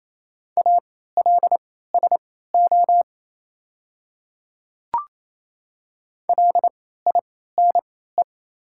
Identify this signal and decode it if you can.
Morse code, heard as a keyed tone, about 27 wpm, spaced as for 16 wpm, 700 Hz. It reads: ALSO LINE
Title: 3x Character Spacing - Top 200 Words - 28wpm